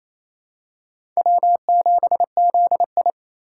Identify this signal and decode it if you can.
Morse code, heard as a keyed tone, about 28 wpm, 690 Hz. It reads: W7ZI
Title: USA Call Signs - 4 Characters - 2x Character Spacing - 28wpm